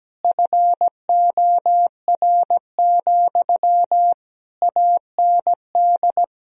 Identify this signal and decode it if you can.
Morse code, heard as a keyed tone, about 17 wpm, 700 Hz. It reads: FOR, AND
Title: Sentences from Top 1000 Words - Review Part 4 - 17wpm